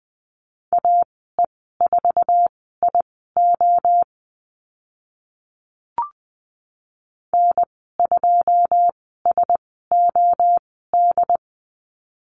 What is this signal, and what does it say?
Morse code, heard as a keyed tone, about 20 wpm, spaced as for 15 wpm, 700 Hz. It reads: AE4IO N2SOD
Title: 082 Lesson 14g - Number 2 - Call Signs 20wpm